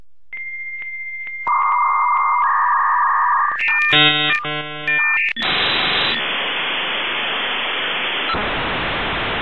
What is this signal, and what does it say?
Dial-up modem handshake.
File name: Der-56k-Modem-Klang-The-56k-dialup-modem-sound